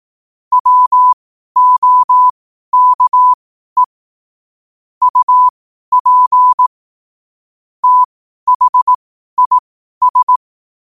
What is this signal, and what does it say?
Morse code, heard as a keyed tone, about 18 wpm, spaced as for 12 wpm, 1000 Hz. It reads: WOKE UP THIS
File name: qotdfiles / 12wpm